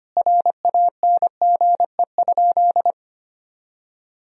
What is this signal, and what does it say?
Morse code, heard as a keyed tone, about 25 wpm, 700 Hz. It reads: RANGE?
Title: Sentences from Top 1000 Words - Review Part 4 - 25wpm